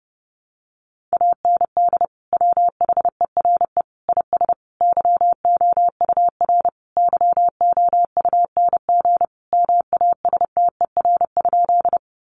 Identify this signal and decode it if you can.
Morse code, heard as a keyed tone, about 30 wpm, 700 Hz. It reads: AND WHERE IS YOUR YOUNG MASTER?
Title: Sentences from Top 1300 Words 30wpm